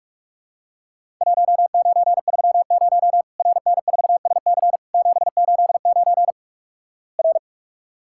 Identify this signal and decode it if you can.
Morse code, heard as a keyed tone, about 48 wpm, 700 Hz. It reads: 1930 RN4SC 789 R
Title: Worked All Europe DX Contest - QTC Exchange - 45wpm